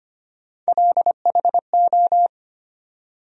Morse code, heard as a keyed tone, about 25 wpm, 700 Hz. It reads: LHO
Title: USA Call Sign Postfixes - 3 Letters 25wpm